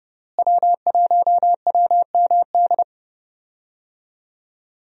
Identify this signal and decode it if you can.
Morse code, heard as a keyed tone, about 30 wpm, 700 Hz. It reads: W1WMD
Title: USA Call Signs - 5 Characters - Repeated 3x - 30wpm